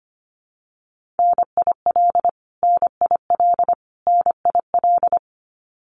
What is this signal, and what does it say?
Morse code, heard as a keyed tone, about 25 wpm, 700 Hz. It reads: NIL NIL NIL
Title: Abbreviations Repeated 3x - 25wpm